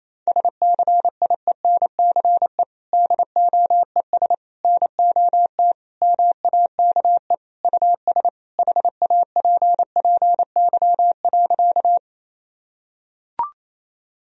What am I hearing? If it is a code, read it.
Morse code, heard as a keyed tone, about 28 wpm, 700 Hz. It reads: SCIENCE DOES NOT MAKE US HAPPY.
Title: Sentences from Top 1000 Words - Review Part 2 - 28wpm